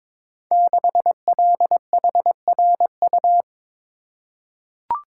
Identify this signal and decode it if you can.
Morse code, heard as a keyed tone, about 21 wpm, 700 Hz. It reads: 6LHRU
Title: Code Groups - 5 Letter:Numbers - 22wpm